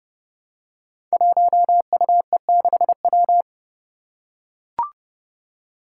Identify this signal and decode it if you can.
Morse code, heard as a keyed tone, about 30 wpm, 700 Hz. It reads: 1UE6W
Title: Five Challenging Letter-Number Combinations 30wpm